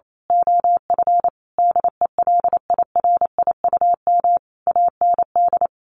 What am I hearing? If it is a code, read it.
Morse code, heard as a keyed tone, about 28 wpm, 700 Hz. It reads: OF DELIRIUM AND
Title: A Sound Of Thunder 28wpm